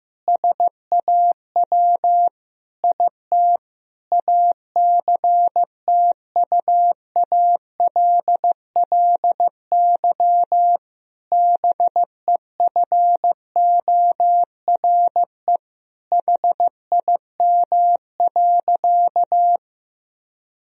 Morse code, heard as a keyed tone, about 15 wpm, 700 Hz. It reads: SAW IT ACTUALLY BEFORE HIM.
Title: Sentences from Top 1000 Words - Review Part 4 - 15wpm